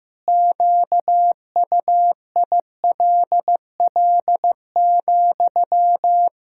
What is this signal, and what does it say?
Morse code, heard as a keyed tone, about 15 wpm, 700 Hz. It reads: QUILL,
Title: Pangram Challenge v2 - 15wpm